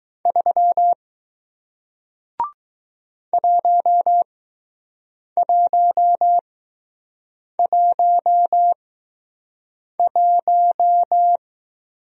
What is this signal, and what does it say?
Morse code, heard as a keyed tone, about 20 wpm, spaced as for 6 wpm, 700 Hz. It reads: 3 1111